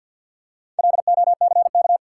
Morse code, heard as a keyed tone, about 52 wpm, 700 Hz. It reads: LOCK
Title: Four Letter Words - 50wpm